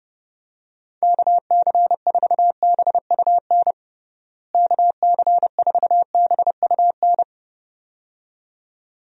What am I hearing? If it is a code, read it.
Morse code, heard as a keyed tone, about 29 wpm, 700 Hz. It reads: KC4BUN KC4BUN
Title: USA Call Signs - 6 Characters - Repeated 2x - 30wpm